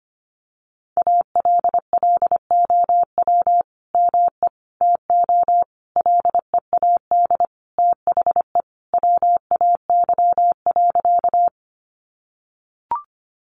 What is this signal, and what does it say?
Morse code, heard as a keyed tone, about 25 wpm, 700 Hz. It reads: ALLOW ME TO LEAD THE WAY.
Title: Sentences from Top 700 Words 25wpm